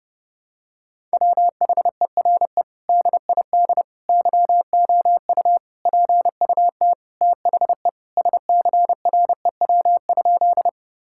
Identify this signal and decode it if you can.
Morse code, heard as a keyed tone, about 32 wpm, 700 Hz. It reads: WHERE DID YOU PUT THE SCREW?
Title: Sentences from Top 1600 Words 30wpm